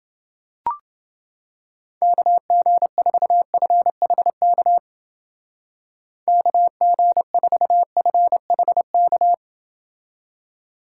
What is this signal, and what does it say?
Morse code, heard as a keyed tone, about 28 wpm, 700 Hz. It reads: KG4FHK KG4FHK